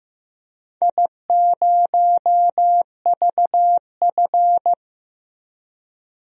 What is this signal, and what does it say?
Morse code, heard as a keyed tone, about 15 wpm, 700 Hz. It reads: I0VF
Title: DXCC-100 Call Signs - 4 Characters - Repeated 3x - 15wpm